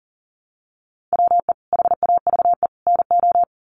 Morse code, heard as a keyed tone, about 37 wpm, 700 Hz. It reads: WE HAVE NO
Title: Sentences from Top 100 Words 40wpm